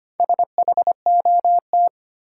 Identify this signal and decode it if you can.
Morse code, heard as a keyed tone, about 25 wpm, 700 Hz. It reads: SHOT
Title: Sentences from Top 1100 Words - New Word Drill Alt - 25wpm